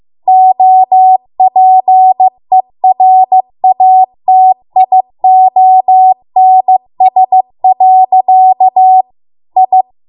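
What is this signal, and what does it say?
Morse code, heard as a keyed tone, about 15 wpm, 750 Hz. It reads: OPERATIONS. I
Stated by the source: Morse code practice at 15 wpm from the national amateur radio association's code archive.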